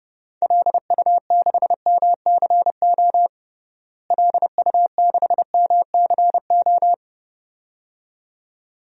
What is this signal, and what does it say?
Morse code, heard as a keyed tone, about 30 wpm, 700 Hz. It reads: LU6MCO LU6MCO
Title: DXCC-100 Call Signs - 6 Characters - Repeated 2x - 30wpm